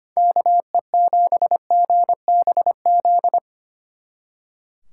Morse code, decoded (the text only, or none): KE7GBZ